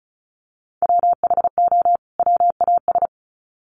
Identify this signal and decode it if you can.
Morse code, heard as a keyed tone, about 34 wpm, 700 Hz. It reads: WHO WAS
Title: Sets of 2 Words 35wpm